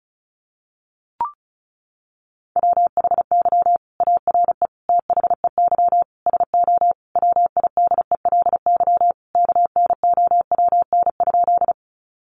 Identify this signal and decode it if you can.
Morse code, heard as a keyed tone, about 34 wpm, 700 Hz. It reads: WHY ARE THEY SO WIDELY KNOWN?
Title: Sentences from Top 1200 Words 35wpm